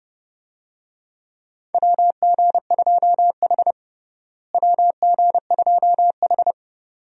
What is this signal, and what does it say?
Morse code, heard as a keyed tone, about 32 wpm, 700 Hz. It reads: WG2H WG2H
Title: USA Call Signs - 4 Characters - Repeated 2x - 30wpm